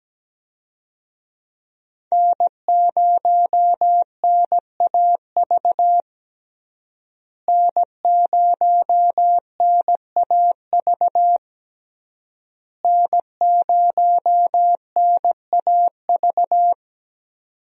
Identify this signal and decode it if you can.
Morse code, heard as a keyed tone, about 17 wpm, 700 Hz. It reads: N0NAV N0NAV N0NAV
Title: USA Call Signs - 5 Characters - Repeated 3x - 17wpm